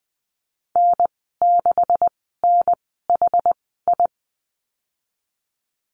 Morse code, heard as a keyed tone, about 20 wpm, spaced as for 15 wpm, 700 Hz. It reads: N6NHI
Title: 147 Lesson 24g - Letter y - Call Signs 20wpm